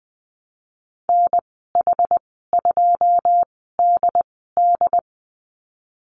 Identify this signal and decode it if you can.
Morse code, heard as a keyed tone, about 20 wpm, spaced as for 15 wpm, 700 Hz. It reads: NH2DD
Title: 161 Lesson 26g - Letter g - Call Signs 20wpm